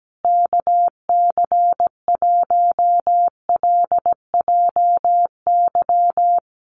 Morse code, heard as a keyed tone, about 17 wpm, 700 Hz. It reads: KC1LJY
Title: USA Call Signs Repeated 3x 17wpm